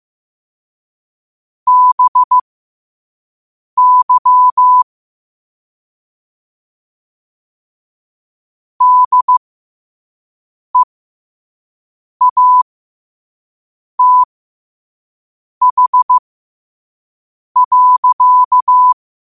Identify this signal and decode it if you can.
Morse code, heard as a keyed tone, about 15 wpm, spaced as for 5 wpm, 1000 Hz. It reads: BY DEATH.
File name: qotdfiles / 5wpm